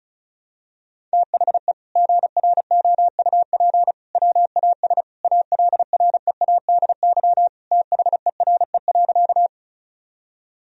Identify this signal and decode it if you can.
Morse code, heard as a keyed tone, about 36 wpm, 700 Hz. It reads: THE GROUP WAS ALREADY THERE.
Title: Sentences from Top 1000 Words - Review Part 1 - 35wpm